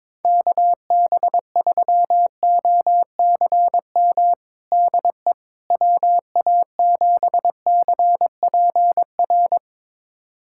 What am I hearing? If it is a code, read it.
Morse code, heard as a keyed tone, about 22 wpm, 700 Hz. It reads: KB3OCM DE WA7CPR
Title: USA Callsign DE Callsign 22wpm